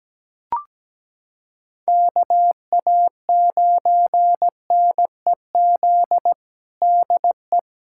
Morse code, heard as a keyed tone, about 17 wpm, 700 Hz. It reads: KA9NEZ DE
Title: USA Callsign DE Callsign 17wpm